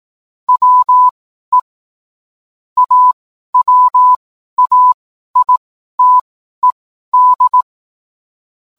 Morse code, heard as a keyed tone, about 18 wpm, spaced as for 12 wpm, 1000 Hz. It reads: WE AWAITED